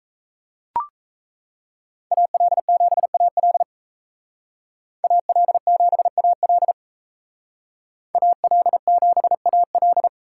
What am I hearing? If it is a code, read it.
Morse code, heard as a keyed tone, about 36 wpm, 700 Hz. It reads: AL7AL AL7AL AL7AL